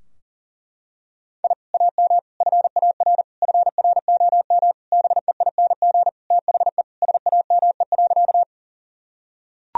Morse code, heard as a keyed tone, about 40 wpm, 700 Hz. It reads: I AM FAR FROM BEING THE SAME.
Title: Sentences from Top 1000 Words - Review Part 3 - 40wpm